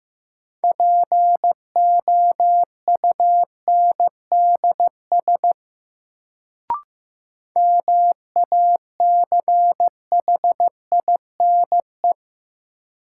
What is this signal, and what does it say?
Morse code, heard as a keyed tone, about 16 wpm, 700 Hz. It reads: POUNDS MACHINE
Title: Sentences from Top 1000 Words - Review Part 2 - New Word Drill - 15wpm